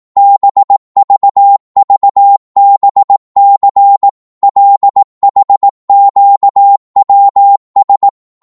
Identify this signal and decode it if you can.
Morse code, heard as a keyed tone, about 18 wpm, 800 Hz. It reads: BVVBC LHQWS
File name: Morse / ltr-18wpm